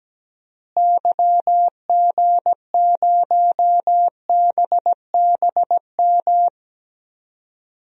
Morse code, heard as a keyed tone, about 17 wpm, 700 Hz. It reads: YG0BBM
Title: DXCC-100 Call Signs - 6 Characters - Repeated 3x - 17wpm